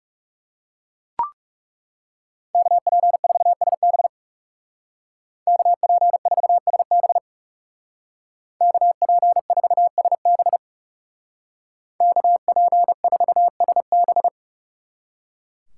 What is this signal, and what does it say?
Morse code, heard as a keyed tone, about 39 wpm, 700 Hz. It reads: KP4SB KP4SB KP4SB KP4SB